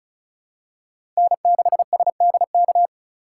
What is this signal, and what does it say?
Morse code, heard as a keyed tone, about 36 wpm, 700 Hz. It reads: N6SDK
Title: USA Call Signs - 5 Characters - 3x Character Spacing - 35wpm